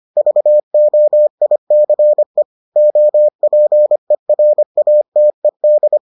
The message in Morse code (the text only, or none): VOICE OPERATED